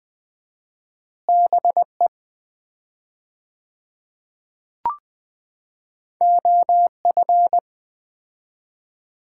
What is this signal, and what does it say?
Morse code, heard as a keyed tone, about 20 wpm, 700 Hz. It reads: BE OF
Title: 4x Word-Spacing - Sets of 2 Words - 20wpm